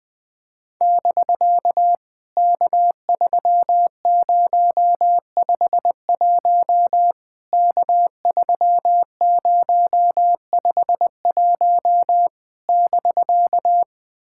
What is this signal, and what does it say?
Morse code, heard as a keyed tone, about 21 wpm, 700 Hz. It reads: <BK> K3051 K3051 <BK>
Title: POTA QSO Practice - 20wpm